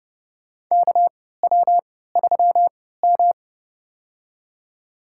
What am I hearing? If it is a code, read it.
Morse code, heard as a keyed tone, about 32 wpm, spaced as for 17 wpm, 700 Hz. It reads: KW3M
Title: USA Call Signs - 4 Characters - 3x Character Spacing - 30wpm